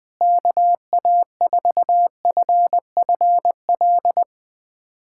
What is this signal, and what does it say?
Morse code, heard as a keyed tone, about 21 wpm, 700 Hz. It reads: KA4FFL
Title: USA Callsign DE Callsign 20wpm